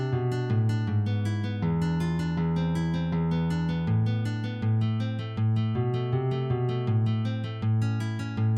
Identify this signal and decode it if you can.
There is no code.